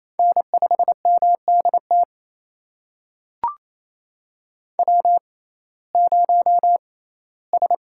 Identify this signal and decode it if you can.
Morse code, heard as a keyed tone, about 28 wpm, spaced as for 17 wpm, 700 Hz. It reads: N5MDT W0S
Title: USA Call Signs - 6x Character Spacing - 28wpm